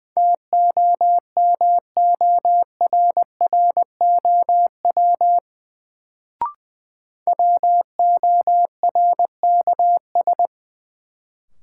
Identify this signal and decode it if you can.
Morse code, heard as a keyed tone, about 20 wpm, 700 Hz. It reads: TOMORROW WORKS
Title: Sentences from Top 1100 Words - New Word Drill Alt - 20wpm